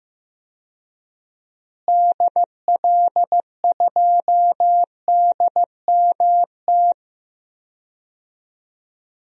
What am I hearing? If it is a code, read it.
Morse code, heard as a keyed tone, about 15 wpm, 700 Hz. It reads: DL2DMT
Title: DXCC-100 Call Signs - 6 Characters - Repeated 2x - 15wpm